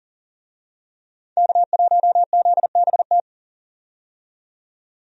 Morse code, heard as a keyed tone, about 40 wpm, 700 Hz. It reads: K1ZBT
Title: USA Call Signs - 5 Characters - 40wpm